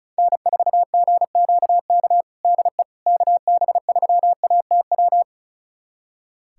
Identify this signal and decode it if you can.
Morse code, heard as a keyed tone, about 34 wpm, 700 Hz. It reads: N4GQK DE KB3ATW
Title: USA Callsign DE Callsign 35wpm